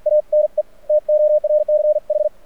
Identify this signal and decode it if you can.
Morse code, heard as a keyed tone, about 48 wpm, spaced as for 42 wpm, 600 Hz. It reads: ANE TORCH
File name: archived / CRSnet